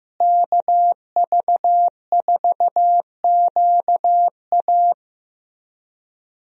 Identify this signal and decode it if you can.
Morse code, heard as a keyed tone, about 15 wpm, 700 Hz. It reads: KV4QA
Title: USA Call Signs - 5 Characters - Repeated 3x - 15wpm